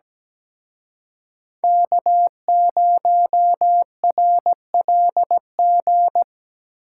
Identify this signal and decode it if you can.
Morse code, heard as a keyed tone, about 20 wpm, 700 Hz. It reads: K0RLG
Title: Speed Racing - USA 1x3 Call Signs 17wpm